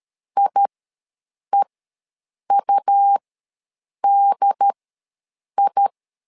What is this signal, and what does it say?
Morse code, heard as a keyed tone, about 13 wpm, spaced as for 7 wpm, 800 Hz. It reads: IEUDI
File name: LEZIONI / 2 UHDN / 2 UHDN ws 12wpm 1708